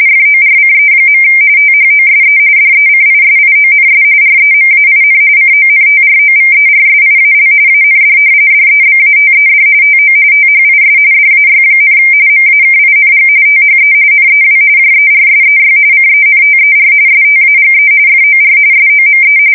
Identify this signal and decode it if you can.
RTTY, about 45.45 baud. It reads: FORCE. THOSE DECISIONS REMAIN CONTROVERSIAL AND DEBATED TO THIS DAY, MAKING THE NEW JERSEY CENTER STAGE FOR SOME
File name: fsk170 / battleshipnj